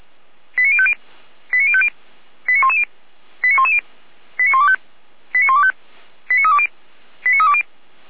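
Five-tone selective call (ZVEI), heard as quick runs of five tones, tones of about 70 ms. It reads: 89050 89050 89110 89110 89125 89125 89320 89320
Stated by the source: ZVEI five-tone selective call 89246 received by SDR.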